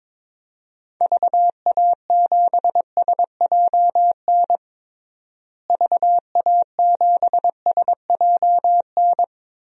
Morse code, heard as a keyed tone, about 22 wpm, 700 Hz. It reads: VA7SJN VA7SJN
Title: DXCC-100 Call Signs - 6 Characters - Repeated 3x - 22wpm